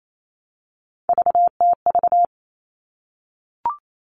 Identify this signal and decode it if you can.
Morse code, heard as a keyed tone, about 28 wpm, 700 Hz. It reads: VTV
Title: All Cut Numbers - 3-Digit 28wpm